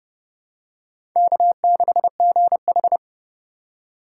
Morse code, heard as a keyed tone, about 30 wpm, 700 Hz. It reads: K6GH
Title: USA Call Signs - 4 Characters - 2x Character Spacing - 30wpm